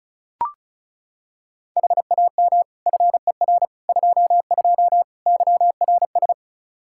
Morse code, heard as a keyed tone, about 36 wpm, 700 Hz. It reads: HAM FER 22 YRS
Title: Practice QSO - Exchange 3 - More Info - 35wpm